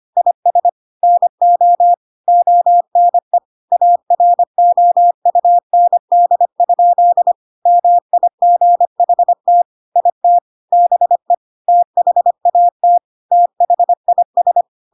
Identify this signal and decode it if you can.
Morse code, heard as a keyed tone, about 25 wpm, 700 Hz. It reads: IS NO ONE AROUND? MIGHT IT BE THAT THIS
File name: Pinocchio / Pinocchio ch1 25 25 ik1hge